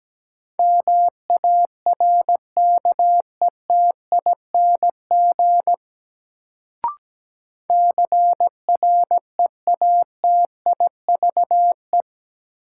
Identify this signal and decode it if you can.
Morse code, heard as a keyed tone, about 17 wpm, 700 Hz. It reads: MARKETING CREATIVE
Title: Sentences from Top 1000 Words - Review Part 5 - New Word Drill - 17wpm